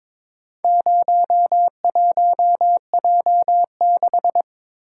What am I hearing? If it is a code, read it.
Morse code, heard as a keyed tone, about 22 wpm, 700 Hz. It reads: 01J6
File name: practice / 22wpm / Four Letter-Number Combinations - 22wpm